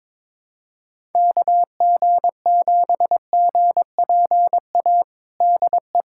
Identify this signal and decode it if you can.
Morse code, heard as a keyed tone, about 22 wpm, 700 Hz. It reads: KG7GPA DE
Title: USA Callsign DE Callsign 22wpm